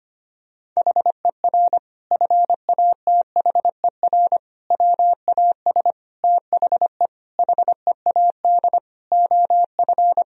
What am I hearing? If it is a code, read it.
Morse code, heard as a keyed tone, about 25 wpm, 700 Hz. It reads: HER FATHER WAS THE HEAD OF
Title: Sentences from Top 1000 Words - Review Part 4 - 25wpm